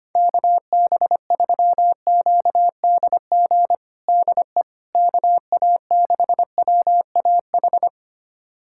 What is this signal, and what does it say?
Morse code, heard as a keyed tone, about 25 wpm, 700 Hz. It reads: KB3QDG DE KA6WAH
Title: USA Callsign DE Callsign - 6 Characters - 25wpm